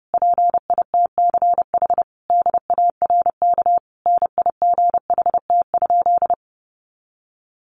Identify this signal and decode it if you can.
Morse code, heard as a keyed tone, about 30 wpm, 700 Hz. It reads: PITCH DARK NIGHT?
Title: Sentences from Top 1300 Words 30wpm